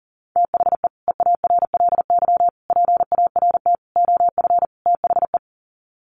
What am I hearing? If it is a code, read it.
Morse code, heard as a keyed tone, about 40 wpm, 700 Hz. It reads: THE EARLY PART OF THE
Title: Sets of 5 Words Encore2 40wpm